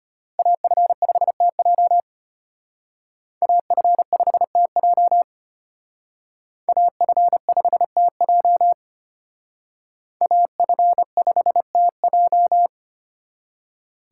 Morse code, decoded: AF5TJ AF5TJ AF5TJ AF5TJ

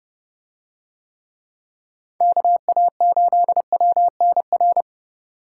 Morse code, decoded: KA8WNR

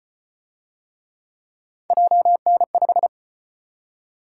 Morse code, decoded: JN5

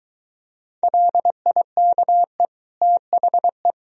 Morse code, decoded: LIKE THE

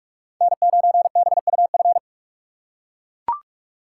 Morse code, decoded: N9BUF

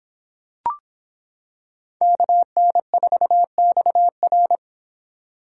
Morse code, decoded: KN4XR

